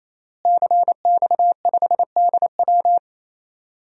Morse code, heard as a keyed tone, about 27 wpm, 700 Hz. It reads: CX5DW
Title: DXCC-100 Call Signs - 5 Characters - Repeated 3x - 28wpm